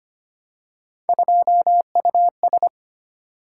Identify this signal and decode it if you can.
Morse code, heard as a keyed tone, about 25 wpm, 700 Hz. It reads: 2US
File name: Three Letter-Number Combinations - 25wpm